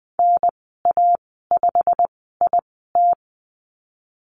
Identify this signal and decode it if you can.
Morse code, heard as a keyed tone, about 20 wpm, spaced as for 14 wpm, 700 Hz. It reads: NA5IT
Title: 088 Lesson 15g - Number 5 - Call Signs 20wpm